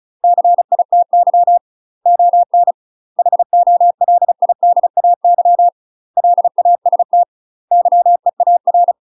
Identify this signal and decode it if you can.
Morse code, heard as a keyed tone, about 34 wpm, 700 Hz. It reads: CITY ON HOLIDAY LAST YEAR
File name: UK MASTER QSOs TRAINING / 35WPM / UK MASTER QSOs 0003